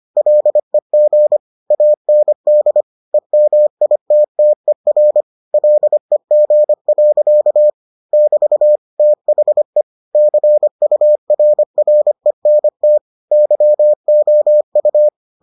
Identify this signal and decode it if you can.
Morse code, heard as a keyed tone, about 25 wpm, 600 Hz. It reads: LEG AND EMITTER LEG. <BT> THE CURRENT YOU